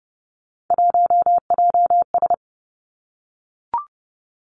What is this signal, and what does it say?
Morse code, heard as a keyed tone, about 30 wpm, 700 Hz. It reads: 1JS
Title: Three Challenging Letter-Number Combinations 30wpm